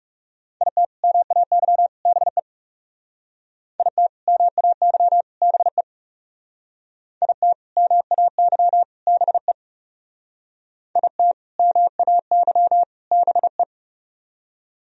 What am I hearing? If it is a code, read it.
Morse code, heard as a keyed tone, about 35 wpm, 700 Hz. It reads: IT MAY BE IT MAY BE IT MAY BE IT MAY BE
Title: Speed-Racing - Sets of 3 Words 30wpm